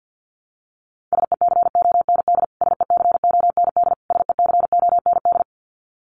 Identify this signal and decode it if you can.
Morse code, heard as a keyed tone, about 50 wpm, 700 Hz. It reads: SECOND SECOND SECOND
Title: Top 500 Words Repeated 50wpm